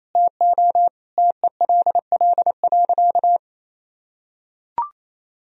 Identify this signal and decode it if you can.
Morse code, heard as a keyed tone, about 28 wpm, 700 Hz. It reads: TO TELL.
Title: Sentences from Top 1000 Words - Review Part 1 - 28wpm